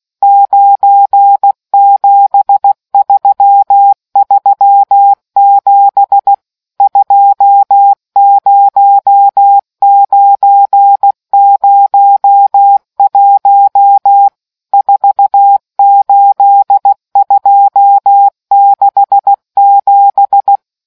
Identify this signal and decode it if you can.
Morse code, heard as a keyed tone, about 16 wpm, 800 Hz. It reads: 97337 20901 48267